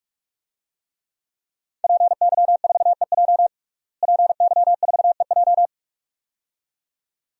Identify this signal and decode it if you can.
Morse code, heard as a keyed tone, about 46 wpm, 700 Hz. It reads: PY4EJ PY4EJ
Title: DXCC-100 Call Signs - 5 Characters - Repeated 2x - 45wpm